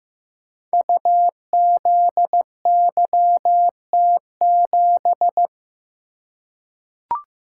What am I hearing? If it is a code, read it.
Morse code, heard as a keyed tone, about 15 wpm, 700 Hz. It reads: UZYT7
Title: Code Groups - 5 Letter:Numbers - 15wpm